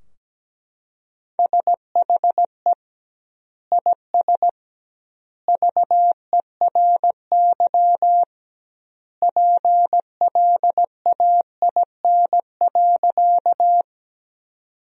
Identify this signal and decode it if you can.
Morse code, heard as a keyed tone, about 17 wpm, 700 Hz. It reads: SHE IS VERY PLAIN.
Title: 2x Word-Spacing - Sentences from Top 400 Words - 17wpm